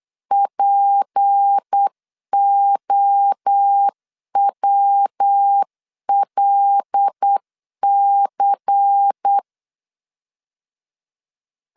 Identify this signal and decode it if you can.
Morse code, heard as a keyed tone, about 8 wpm, 800 Hz. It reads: POWLC
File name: CW / DET / 1 6 1500C S C 8wpm 5219